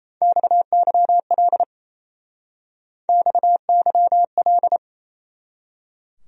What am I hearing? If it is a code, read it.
Morse code, heard as a keyed tone, about 36 wpm, 700 Hz. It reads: XYL XYL